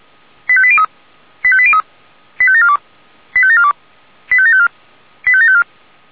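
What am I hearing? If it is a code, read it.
Five-tone selective call (ZVEI), heard as quick runs of five tones, tones of about 70 ms. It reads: 85803 85803 85742 85742 85754 85754
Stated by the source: ZVEI five-tone call received on an SDR.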